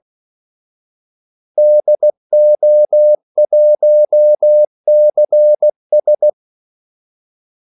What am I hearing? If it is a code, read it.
Morse code, heard as a keyed tone, about 16 wpm, 600 Hz. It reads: DO1CS
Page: Kaldesignaler 16 wpm